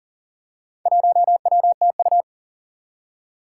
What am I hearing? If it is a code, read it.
Morse code, heard as a keyed tone, about 38 wpm, 700 Hz. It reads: 1WTU
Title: Four Challenging Letter-Number Combinations 40wpm